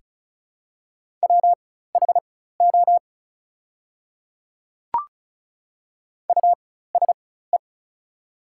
Morse code, heard as a keyed tone, about 36 wpm, spaced as for 16 wpm, 700 Hz. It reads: WHO USE
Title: 4x Character Spacing - Top 200 Words - 35wpm